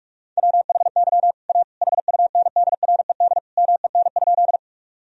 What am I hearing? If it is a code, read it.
Morse code, heard as a keyed tone, about 43 wpm, 700 Hz. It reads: WHY A HUNDRED MEN?
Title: Sentences from Top 1000 Words - Review Part 2 - 45wpm